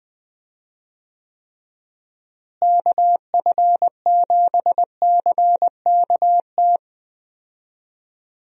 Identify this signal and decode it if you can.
Morse code, heard as a keyed tone, about 21 wpm, 700 Hz. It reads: KF7CKT
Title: USA Call Signs - 6 Characters - Repeated 2x - 20wpm